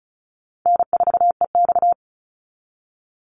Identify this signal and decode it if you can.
Morse code, heard as a keyed tone, about 34 wpm, 700 Hz. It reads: N4EX
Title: NA SOTA Activator Callsigns 35wpm